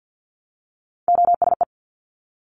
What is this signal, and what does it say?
Morse code, heard as a keyed tone, about 50 wpm, 700 Hz. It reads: OSE
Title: Top Three Letter Combinations in Words 50wpm